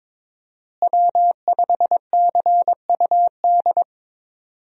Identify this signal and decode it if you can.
Morse code, heard as a keyed tone, about 22 wpm, 700 Hz. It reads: W5CUD
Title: USA Call Signs - 5 Characters - Repeated 2x - 22wpm